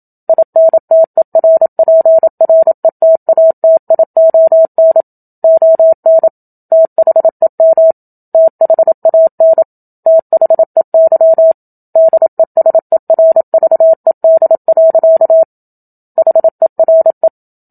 Morse code, heard as a keyed tone, about 27 wpm, 650 Hz. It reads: INTERPRETATION ON THEM THAN THEY DESERVED. HERE
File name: Book Two / 01-Chapter 1 - 27 WPM